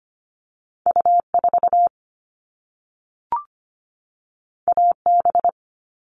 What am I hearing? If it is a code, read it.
Morse code, heard as a keyed tone, about 25 wpm, 700 Hz. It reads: U4 AB
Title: All Cut Numbers - 2-Digit 25wpm